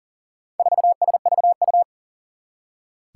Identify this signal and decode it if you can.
Morse code, heard as a keyed tone, about 40 wpm, 700 Hz. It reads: 4SVU
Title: Four Challenging Letter-Number Combinations 40wpm